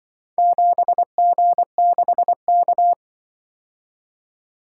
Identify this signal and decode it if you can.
Morse code, heard as a keyed tone, about 25 wpm, 700 Hz. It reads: 7G6K
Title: Speed-Racing - Four Letter-Number Combinations - 20wpm